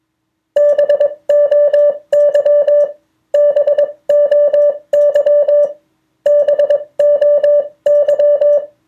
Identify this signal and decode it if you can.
Morse code, heard as a keyed tone, about 21 wpm, spaced as for 19 wpm, 580 Hz. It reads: BOY BOY BOY
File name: 2of4-words-1